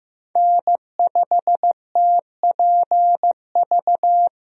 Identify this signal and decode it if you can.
Morse code, heard as a keyed tone, about 15 wpm, 700 Hz. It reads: N5TPV
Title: USA Call Signs - 5 Characters - Repeated 2x - 15wpm